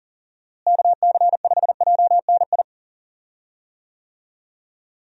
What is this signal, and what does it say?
Morse code, heard as a keyed tone, about 38 wpm, 700 Hz. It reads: KC5JNI
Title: USA Call Signs - 6 Characters - Repeated 3x - 40wpm